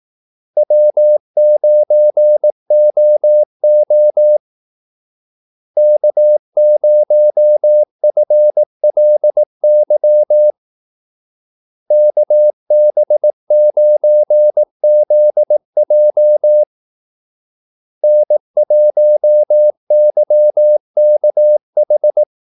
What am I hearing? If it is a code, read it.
Morse code, heard as a keyed tone, about 18 wpm, 600 Hz. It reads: W9OO K0FLY KB9ZJ N1YKH
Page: Kaldesignaler 18 wpm